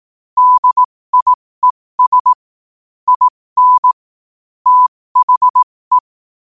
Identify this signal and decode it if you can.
Morse code, heard as a keyed tone, about 18 wpm, spaced as for 15 wpm, 1000 Hz. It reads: DIES IN THE
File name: qotdfiles / 15wpm